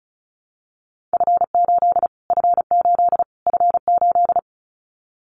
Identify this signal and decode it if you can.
Morse code, heard as a keyed tone, about 34 wpm, 700 Hz. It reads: F8 F8 F8
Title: Callsign Prefixes for Top 25 DXCC - Repeated 3x - 35wpm